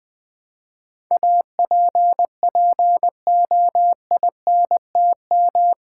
Morse code, heard as a keyed tone, about 20 wpm, 700 Hz. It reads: APPOINTM
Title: Build A Word - 4 Letters At A Time - 11-Letter Words - 20wpm